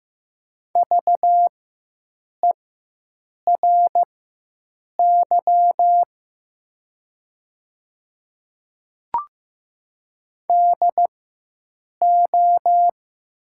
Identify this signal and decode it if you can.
Morse code, heard as a keyed tone, about 15 wpm, spaced as for 7 wpm, 700 Hz. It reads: VERY DO
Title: 4x Character Spacing - Top 200 Words - 15wpm